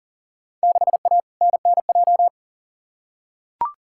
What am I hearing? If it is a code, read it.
Morse code, heard as a keyed tone, about 39 wpm, 700 Hz. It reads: 6A NNJ
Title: Field Day Exchanges Encore - 40wpm